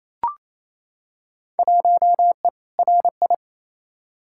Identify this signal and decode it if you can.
Morse code, heard as a keyed tone, about 28 wpm, 700 Hz. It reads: 1E RI
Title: Field Day Exchanges Encore - 28wpm